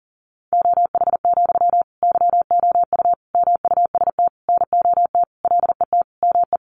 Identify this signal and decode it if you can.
Morse code, heard as a keyed tone, about 40 wpm, 700 Hz. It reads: OH, YOU MUST NOT LET ME
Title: Sentences from Top 1100 Words 40wpm